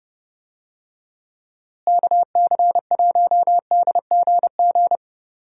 Morse code, heard as a keyed tone, about 30 wpm, 700 Hz. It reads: KC1DGG
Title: USA Call Signs - 6 Characters - Repeated 2x - 30wpm